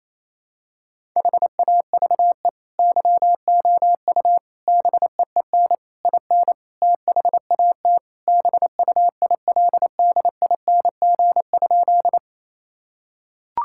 Morse code, heard as a keyed tone, about 28 wpm, 700 Hz. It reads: HAVE YOU BEEN IN THAT BUILDING?
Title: Sentences from Top 1000 Words - Review Part 5 - 28wpm